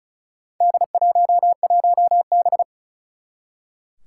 Morse code, heard as a keyed tone, about 36 wpm, 700 Hz. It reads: D11B
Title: Four Challenging Letter-Number Combinations 35wpm